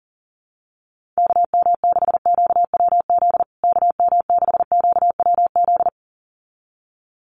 Morse code, heard as a keyed tone, about 40 wpm, 700 Hz. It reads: KM6QWZ KM6QWZ
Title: USA Call Signs Repeated 2x 40wpm